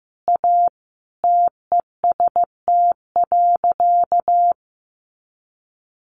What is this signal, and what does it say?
Morse code, heard as a keyed tone, about 14 wpm, 700 Hz. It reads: A TEST.
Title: Additional Easy Phrases 15wpm